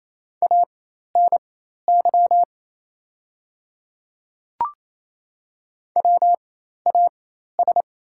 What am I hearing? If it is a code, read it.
Morse code, heard as a keyed tone, about 27 wpm, spaced as for 13 wpm, 690 Hz. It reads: ANY WAS
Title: 4x Character Spacing - Top 200 Words - 28wpm